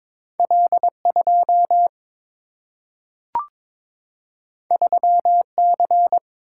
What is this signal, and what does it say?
Morse code, heard as a keyed tone, about 22 wpm, 700 Hz. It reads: L2 3C
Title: Single-Letter Single-Number 22wpm